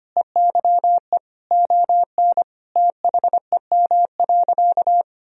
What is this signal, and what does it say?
Morse code, heard as a keyed tone, about 25 wpm, 700 Hz. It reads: EYE ON THEM.
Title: Sentences from Top 1000 Words - Review Part 1 - 25wpm